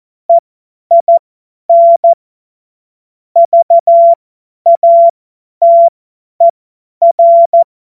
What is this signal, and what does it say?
Morse code, heard as a keyed tone, about 14 wpm, spaced as for 10 wpm, 680 Hz. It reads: EIN VATER
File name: Morsekurs / Witze / Witz 10. Januar 2023 50 F70